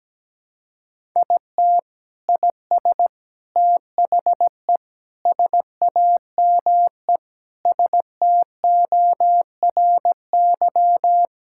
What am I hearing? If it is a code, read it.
Morse code, heard as a keyed tone, about 17 wpm, 700 Hz. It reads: IT IS THE SAME STORY
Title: Sentences from Top 1600 Words 17wpm